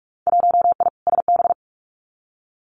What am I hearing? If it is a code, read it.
Morse code, heard as a keyed tone, about 44 wpm, 700 Hz. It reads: 1I SB
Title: Winter Field Day Exchanges 45wpm